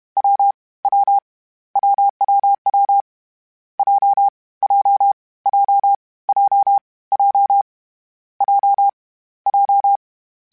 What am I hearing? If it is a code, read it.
Morse code, heard as a keyed tone, about 31 wpm, 800 Hz. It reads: W W WWW J J J J J J J
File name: Difficult Combos / Difficult Combos-mp3s / AWJ1 @30